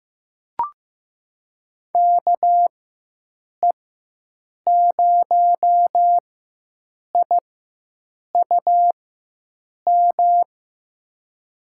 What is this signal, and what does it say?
Morse code, heard as a keyed tone, about 15 wpm, spaced as for 7 wpm, 700 Hz. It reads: KE0IUM
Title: USA Call Signs - 6 Characters - 4x Character Spacing - 15wpm